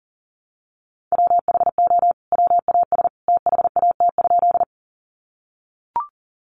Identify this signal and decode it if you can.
Morse code, heard as a keyed tone, about 40 wpm, 700 Hz. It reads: WHO WAS THAT?
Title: Sentences from Top 100 Words 40wpm